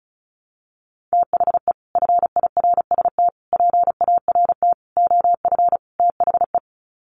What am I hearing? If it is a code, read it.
Morse code, heard as a keyed tone, about 34 wpm, 700 Hz. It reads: THE FIRST PART OF THE
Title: Sets of 5 Words Encore 35wpm